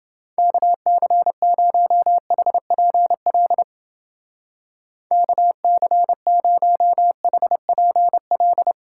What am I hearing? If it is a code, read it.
Morse code, heard as a keyed tone, about 30 wpm, 700 Hz. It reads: KC0HPL KC0HPL